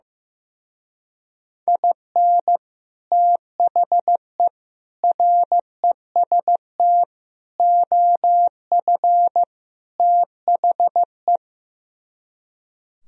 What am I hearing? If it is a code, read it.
Morse code, heard as a keyed tone, about 15 wpm, 700 Hz. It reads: IN THE REST OF THE
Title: Speed-Racing - Sets of 5 Words 15wpm